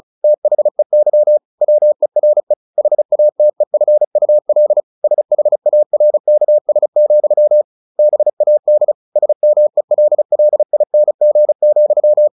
Morse code, heard as a keyed tone, about 34 wpm, 600 Hz. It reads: THEY WERE HATEFUL SHARKS, BAD SMELLING,
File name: the old man and the sea 2 35wpm